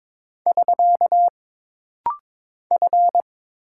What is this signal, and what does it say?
Morse code, heard as a keyed tone, about 22 wpm, 700 Hz. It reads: <SK> F
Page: This rapid-fire timing is geared to push your ICR (Instant Character Recognition) skills.